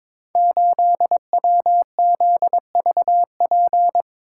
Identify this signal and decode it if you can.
Morse code, heard as a keyed tone, about 22 wpm, 700 Hz. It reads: 8WZVP
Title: Code Groups - 5 Letter:Numbers - 22wpm